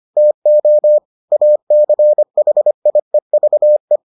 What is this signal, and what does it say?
Morse code, heard as a keyed tone, about 25 wpm, 600 Hz. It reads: TO ACHIEVE